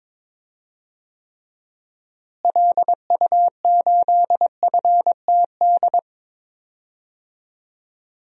Morse code, heard as a keyed tone, about 22 wpm, 700 Hz. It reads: LU8FTD
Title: DXCC-100 Call Signs - 6 Characters - Repeated 2x - 22wpm